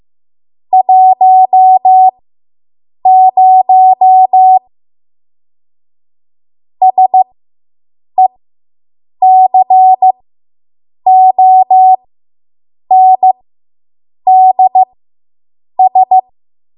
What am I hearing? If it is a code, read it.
Morse code, heard as a keyed tone, about 15 wpm, spaced as for 7 wpm, 750 Hz. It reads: 10 SECONDS